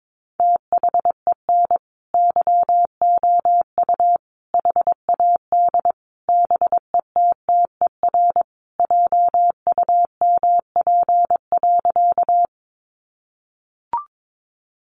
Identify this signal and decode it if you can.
Morse code, heard as a keyed tone, about 22 wpm, 700 Hz. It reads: THEN YOU HAD BETTER JUMP.
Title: Sentences from Top 1300 Words 22wpm